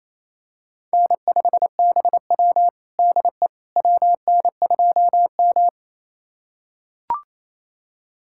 Morse code, heard as a keyed tone, about 27 wpm, 700 Hz. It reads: N5BW DE WN2M
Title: USA Callsign DE Callsign - 4 Characters - 28wpm